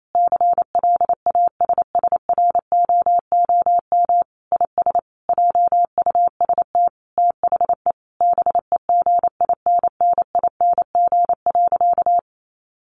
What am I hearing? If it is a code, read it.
Morse code, heard as a keyed tone, about 28 wpm, 700 Hz. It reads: CLASSROOM IS JUST THE BEGINNING.
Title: Sentences from Top 1500 Words 28wpm